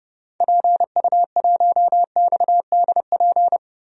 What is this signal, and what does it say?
Morse code, heard as a keyed tone, about 30 wpm, 700 Hz. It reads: PU1XDP
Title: DXCC-100 Call Signs - 6 Characters - 30wpm